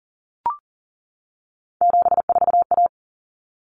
Morse code, heard as a keyed tone, about 40 wpm, 700 Hz. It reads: 74A
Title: Common Cut Numbers - 3-Digit 40wpm